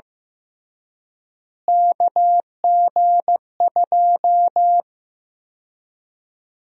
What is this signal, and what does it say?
Morse code, heard as a keyed tone, about 15 wpm, 700 Hz. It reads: KG2
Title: USA Call Sign Prefixes - Repeated 2x - 15wpm